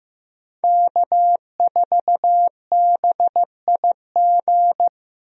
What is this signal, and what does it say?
Morse code, heard as a keyed tone, about 15 wpm, 700 Hz. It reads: K4BIG
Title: USA Call Signs - 5 Characters - 15wpm